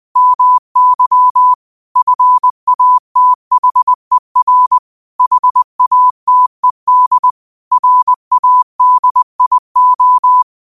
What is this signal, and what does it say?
Morse code, heard as a keyed tone, about 20 wpm, 1000 Hz. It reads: MY FATHER HATED RADIO
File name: qotdfiles / 20wpm